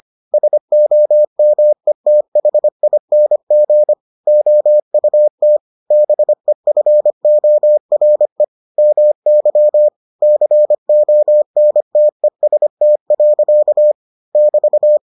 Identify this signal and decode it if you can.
Morse code, heard as a keyed tone, about 25 wpm, 600 Hz. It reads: SOMETHING OUT BEFORE MY CONTEST. <BT>